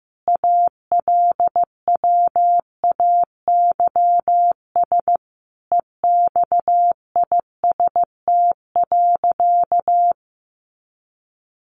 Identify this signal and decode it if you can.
Morse code, heard as a keyed tone, about 15 wpm, 700 Hz. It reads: ALWAYS EXIST.
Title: Sentences from Top 1500 Words 15wpm